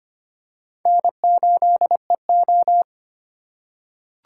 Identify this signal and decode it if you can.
Morse code, heard as a keyed tone, about 25 wpm, 700 Hz. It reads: N8EO
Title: USA Call Signs - 4 Characters - 2x Character Spacing - 25wpm